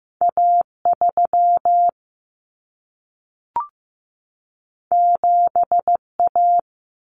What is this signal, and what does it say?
Morse code, heard as a keyed tone, about 15 wpm, 700 Hz. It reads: A 3 7 A
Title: Common Cut Numbers - 2-Digit 15wpm